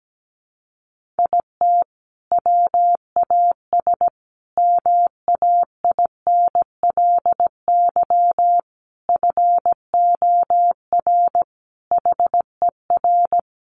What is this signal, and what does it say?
Morse code, heard as a keyed tone, about 17 wpm, 700 Hz. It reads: IT WAS MAINLY FOR HER
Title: Sentences from Top 1400 Words 17wpm